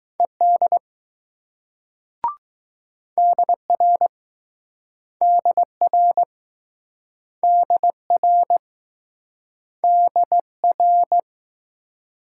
Morse code, heard as a keyed tone, about 20 wpm, 700 Hz. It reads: ED DR DR DR DR